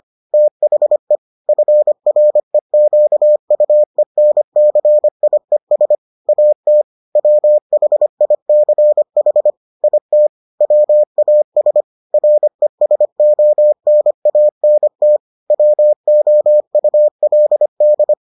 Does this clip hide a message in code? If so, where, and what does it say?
Yes: Morse code from 0.3 to 18.1 s, heard as a keyed tone, about 25 wpm, 600 Hz. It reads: THE FREQUENCIES AT WHICH IT WAS RESONANT WOULD